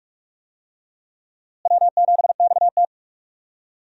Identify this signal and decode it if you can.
Morse code, heard as a keyed tone, about 44 wpm, 700 Hz. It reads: W7XT
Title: USA Call Signs - 4 Characters - Repeated 2x - 45wpm